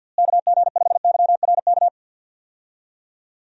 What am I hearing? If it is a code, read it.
Morse code, heard as a keyed tone, about 48 wpm, 700 Hz. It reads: KG5YRK
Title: USA Call Signs - 6 Characters - Repeated 3x - 50wpm